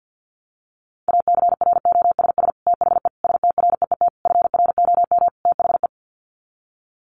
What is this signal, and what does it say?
Morse code, heard as a keyed tone, about 50 wpm, 700 Hz. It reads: ACROSS THE STREET FROM THE
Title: Sets of 5 Words Encore 50wpm